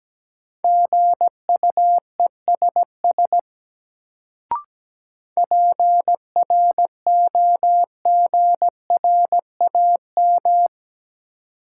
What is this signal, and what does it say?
Morse code, heard as a keyed tone, about 17 wpm, 700 Hz. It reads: GUESS PROGRAM
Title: Sentences from Top 1500 Words - New Word Drill Alt - 17wpm